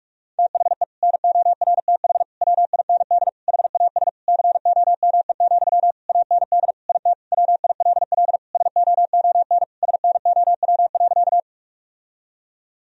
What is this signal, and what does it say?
Morse code, heard as a keyed tone, about 48 wpm, 700 Hz. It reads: THE NORTH WIND HAS COME, AND IT WILL SOON SNOW.
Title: Sentences from Top 1000 Words - Review Part 3 - 45wpm